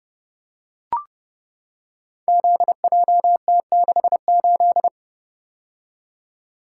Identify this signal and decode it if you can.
Morse code, heard as a keyed tone, about 31 wpm, 700 Hz. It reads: ZJT68
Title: Code Groups - 5 Letter:Numbers - 30wpm